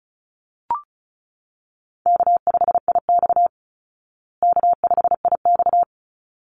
Morse code, heard as a keyed tone, about 34 wpm, 700 Hz. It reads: K5IX K5IX
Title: Call Signs Repeated 3x - 35wpm